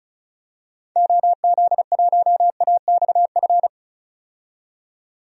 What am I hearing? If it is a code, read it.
Morse code, heard as a keyed tone, about 36 wpm, 700 Hz. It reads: OZ1AXF
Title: DXCC-100 Call Signs - 6 Characters - 35wpm